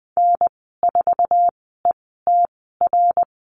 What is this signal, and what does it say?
Morse code, heard as a keyed tone, about 20 wpm, spaced as for 14 wpm, 700 Hz. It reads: N4ETR
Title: 088 Lesson 15g - Number 5 - Call Signs 20wpm